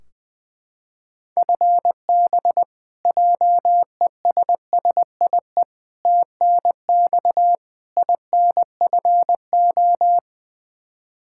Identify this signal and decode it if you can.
Morse code, heard as a keyed tone, about 20 wpm, 700 Hz. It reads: FB JESSIE TNX INFO
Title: Practice QSO - Exchange 3 - More Info - 20wpm